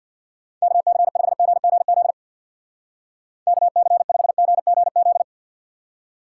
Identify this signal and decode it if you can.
Morse code, heard as a keyed tone, about 62 wpm, 700 Hz. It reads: KC5GGZ KC5GGZ
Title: Speed Racing - USA 2x3 Call Signs 50wpm